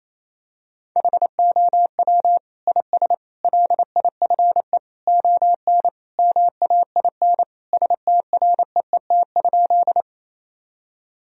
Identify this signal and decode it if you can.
Morse code, heard as a keyed tone, about 28 wpm, 700 Hz. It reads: HOW IS LIFE ON MAIN STREET?
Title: Sentences from Top 1000 Words - Review Part 2 - 28wpm